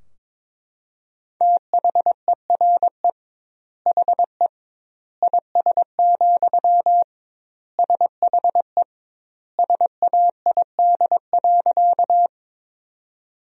Morse code, heard as a keyed tone, about 22 wpm, 700 Hz. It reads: THERE HE IS, SHE SAID.
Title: 2x Word-Spacing - Sentences from Top 100 Words - 22wpm